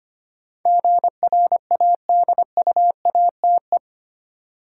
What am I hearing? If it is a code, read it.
Morse code, heard as a keyed tone, about 25 wpm, 690 Hz. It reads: GRADUATE
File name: 8 Letter Words - First 3 Letters Given - 25wpm